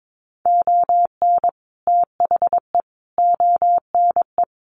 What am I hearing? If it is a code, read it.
Morse code, heard as a keyed tone, about 22 wpm, 700 Hz. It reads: ON THE ONE
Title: Sets of 4 Words 22wpm